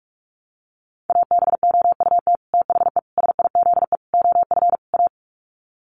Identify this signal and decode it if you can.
Morse code, heard as a keyed tone, about 44 wpm, 700 Hz. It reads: ABOUT THE SIZE OF A
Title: Sets of 5 Words 45wpm